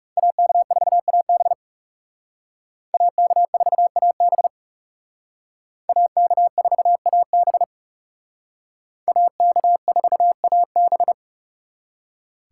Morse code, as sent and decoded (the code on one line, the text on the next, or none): .- -.- ....- .- -... / .- -.- ....- .- -... / .- -.- ....- .- -... / .- -.- ....- .- -...
AK4AB AK4AB AK4AB AK4AB